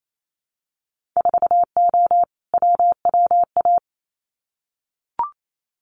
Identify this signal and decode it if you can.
Morse code, heard as a keyed tone, about 28 wpm, 700 Hz. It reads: 4O WWA
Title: Winter Field Day Exchanges 28wpm